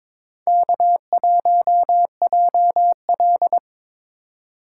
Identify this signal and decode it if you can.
Morse code, heard as a keyed tone, about 26 wpm, 700 Hz. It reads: K1JL
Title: Speed Racing - USA 1x2 Call Signs 22wpm